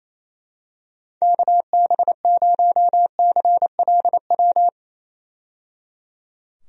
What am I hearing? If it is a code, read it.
Morse code, heard as a keyed tone, about 27 wpm, 700 Hz. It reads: KB0CLW
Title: USA Call Signs - 6 Characters - 28wpm